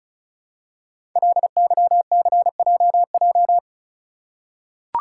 Morse code, heard as a keyed tone, about 36 wpm, 690 Hz. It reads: LYCJJ
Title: Code Groups - 5 Letter:Numbers - 35wpm